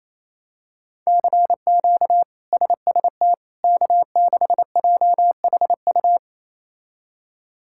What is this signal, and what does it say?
Morse code, heard as a keyed tone, about 27 wpm, 700 Hz. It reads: CQ SST K6JHU
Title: K1USN SST QSO Practice - 28wpm